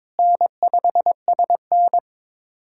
Morse code, heard as a keyed tone, about 21 wpm, 700 Hz. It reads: N5SN
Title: USA Callsign DE Callsign - 4 Characters - 22wpm